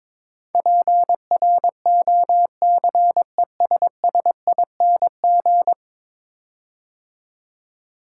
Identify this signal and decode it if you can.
Morse code, heard as a keyed tone, about 21 wpm, 700 Hz. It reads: PROCESSING
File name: practice / 22wpm / 10 Letter Words - First 3 Letters Given - 22wpm